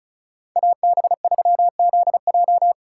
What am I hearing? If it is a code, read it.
Morse code, heard as a keyed tone, about 36 wpm, 700 Hz. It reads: AB3ZJ
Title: USA Call Signs - 5 Characters - 2x Character Spacing - 35wpm